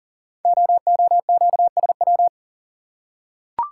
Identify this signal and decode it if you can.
Morse code, heard as a keyed tone, about 39 wpm, 700 Hz. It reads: OOQSW
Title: Code Groups - 5 Letters - 40wpm